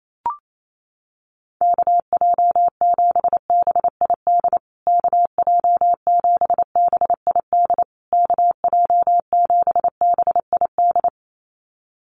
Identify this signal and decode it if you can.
Morse code, heard as a keyed tone, about 28 wpm, 700 Hz. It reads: KJ7BID KJ7BID KJ7BID
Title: USA Call Signs Repeated 3x 28wpm